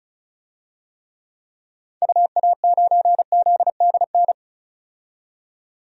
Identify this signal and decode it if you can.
Morse code, heard as a keyed tone, about 35 wpm, 700 Hz. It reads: UA9ZDN
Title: DXCC-100 Call Signs - 6 Characters - 35wpm